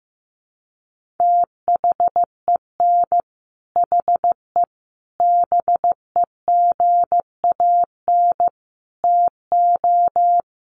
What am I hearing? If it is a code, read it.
Morse code, heard as a keyed tone, about 15 wpm, 700 Hz. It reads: THEN HE BEGAN TO
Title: Sentences from Top 1200 Words 15wpm